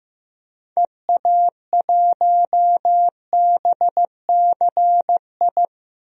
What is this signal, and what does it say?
Morse code, heard as a keyed tone, about 15 wpm, 700 Hz. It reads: EA1BCI
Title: DXCC-100 Call Signs - 6 Characters - Repeated 3x - 15wpm